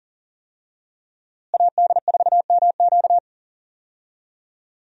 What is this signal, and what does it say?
Morse code, heard as a keyed tone, about 40 wpm, 700 Hz. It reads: AD4MQ
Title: USA Call Signs - 5 Characters - Repeated 3x - 40wpm